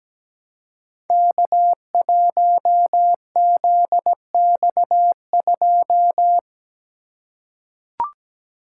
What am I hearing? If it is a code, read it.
Morse code, heard as a keyed tone, about 17 wpm, 700 Hz. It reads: K1ZX2
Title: Code Groups - 5 Letter:Numbers - 17wpm